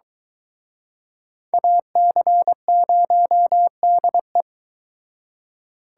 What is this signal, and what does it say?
Morse code, heard as a keyed tone, about 23 wpm, 700 Hz. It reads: AC0DE